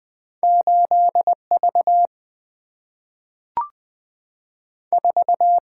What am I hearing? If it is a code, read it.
Morse code, heard as a keyed tone, about 20 wpm, 700 Hz. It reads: 8 V 4
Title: Single-Letter Single-Number 20wpm